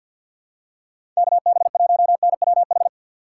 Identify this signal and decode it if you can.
Morse code, heard as a keyed tone, about 52 wpm, 700 Hz. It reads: KB1NWH
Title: USA Call Signs - 6 Characters - 6x Character Spacing - 50wpm